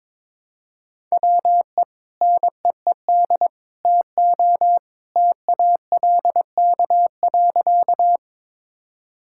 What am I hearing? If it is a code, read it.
Morse code, heard as a keyed tone, about 22 wpm, 700 Hz. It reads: WE NEED TO TALK.
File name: practice / 22wpm / Sentences with 2-Word Phrases - 22wpm